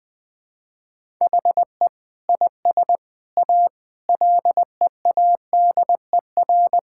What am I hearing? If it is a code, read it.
Morse code, heard as a keyed tone, about 20 wpm, 700 Hz. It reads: HE IS A LEADER
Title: Sentences with 6-Word Phrases - 20wpm